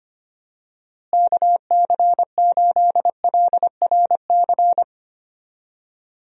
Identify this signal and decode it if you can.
Morse code, heard as a keyed tone, about 25 wpm, 700 Hz. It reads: KC8LRC
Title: USA Call Signs - 6 Characters - Repeated 3x - 25wpm